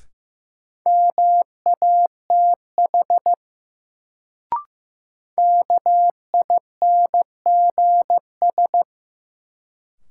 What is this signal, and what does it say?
Morse code, heard as a keyed tone, about 15 wpm, 700 Hz. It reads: MATH KINGS
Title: Sentences from Top 1000 Words - Review Part 6 - New Word Drill - 15wpm